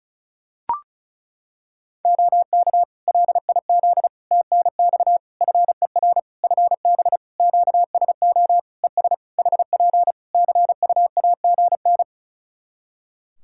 Morse code, heard as a keyed tone, about 36 wpm, 700 Hz. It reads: OK LIZ TNX FER FB QSO ES HP CUAGN
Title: K1USN SST QSO Practice 35wpm